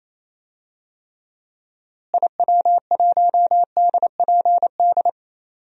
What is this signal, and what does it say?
Morse code, heard as a keyed tone, about 28 wpm, 700 Hz. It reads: IW1DPD
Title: DXCC-100 Call Signs - 6 Characters - Repeated 2x - 28wpm